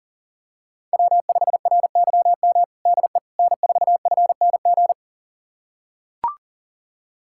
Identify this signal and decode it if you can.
Morse code, heard as a keyed tone, about 37 wpm, 700 Hz. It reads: W5RYM DE N4FNG
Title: USA Callsign DE Callsign - 5 Characters - 40wpm